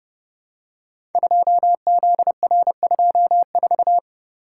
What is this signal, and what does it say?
Morse code, heard as a keyed tone, about 30 wpm, 700 Hz. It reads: 2ZR24
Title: Code Groups - 5 Letter:Numbers - 30wpm